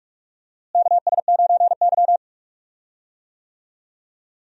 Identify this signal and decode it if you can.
Morse code, heard as a keyed tone, about 46 wpm, 700 Hz. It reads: KS9Y
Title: USA Call Signs - 4 Characters - Repeated 3x - 45wpm